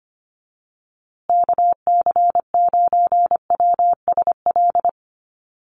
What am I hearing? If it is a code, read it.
Morse code, heard as a keyed tone, about 25 wpm, 700 Hz. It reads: KC9WSL
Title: USA Call Signs Repeated 2x 25wpm